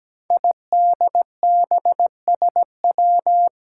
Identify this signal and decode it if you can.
Morse code, heard as a keyed tone, about 17 wpm, 700 Hz. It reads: IDBSW
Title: Five Challenging Letter-Number Combinations 17wpm